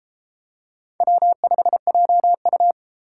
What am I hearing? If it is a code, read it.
Morse code, heard as a keyed tone, about 32 wpm, 700 Hz. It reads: W5JU